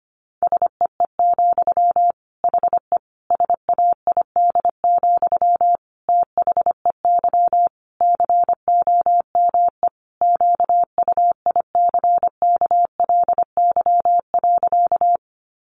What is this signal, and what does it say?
Morse code, heard as a keyed tone, about 25 wpm, 700 Hz. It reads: SEE, HE SAID, THEY COME QUICKLY.
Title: Sentences from Top 700 Words 25wpm